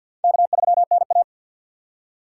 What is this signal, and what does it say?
Morse code, heard as a keyed tone, about 52 wpm, 700 Hz. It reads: K3NA
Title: USA Call Signs - 4 Characters - 3x Character Spacing - 50wpm